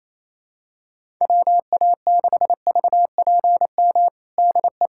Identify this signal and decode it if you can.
Morse code, heard as a keyed tone, about 28 wpm, 700 Hz. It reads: WA6VPM DE
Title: USA Callsign DE Callsign - 6 Characters - 28wpm